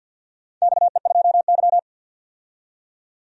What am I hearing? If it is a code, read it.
Morse code, heard as a keyed tone, about 47 wpm, 700 Hz. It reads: XE2Y
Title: DXCC-100 Call Signs - 4 Characters - 50wpm